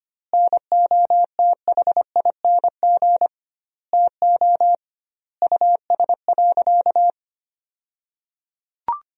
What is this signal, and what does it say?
Morse code, heard as a keyed tone, about 25 wpm, 700 Hz. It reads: NOTHING TO US.
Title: 2x Word-Spacing - Sentences from Top 500 Words - 25wpm